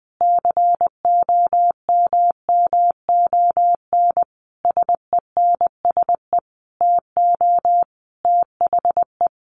Morse code, heard as a keyed tone, about 20 wpm, 700 Hz. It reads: COMMON SENSE TO THE
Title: Sentences from Top 1200 Words 20wpm